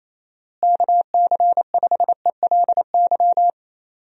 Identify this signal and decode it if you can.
Morse code, heard as a keyed tone, about 28 wpm, 700 Hz. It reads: KC5ELY
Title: USA Call Signs - 6 Characters - Repeated 2x - 28wpm